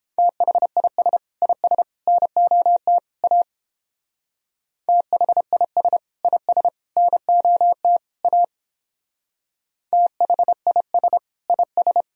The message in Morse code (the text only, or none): THIS IS NOT A THIS IS NOT A THIS IS